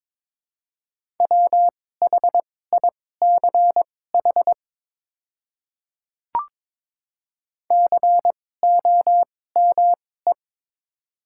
Morse code, heard as a keyed tone, about 22 wpm, spaced as for 16 wpm, 690 Hz. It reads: WHICH COME
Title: 2x Character Spacing - Top 100 Words - 22wpm